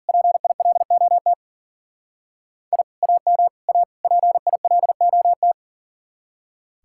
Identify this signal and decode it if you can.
Morse code, heard as a keyed tone, about 50 wpm, 700 Hz. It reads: PILOT I AM A PILOT
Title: Speed-Racing Common QSO Phrases - 40wpm